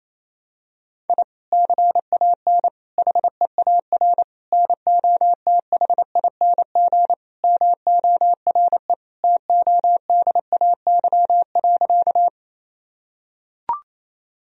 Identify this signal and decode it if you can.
Morse code, heard as a keyed tone, about 28 wpm, 700 Hz. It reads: I CAN HEAR NOTHING MORE TODAY.
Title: Sentences from Top 1000 Words - Review Part 4 - 28wpm